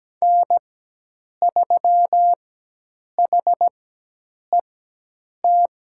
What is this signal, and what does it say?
Morse code, heard as a keyed tone, about 17 wpm, spaced as for 8 wpm, 700 Hz. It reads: N3HET
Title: USA Call Signs - 5 Characters - 4x Character Spacing - 17wpm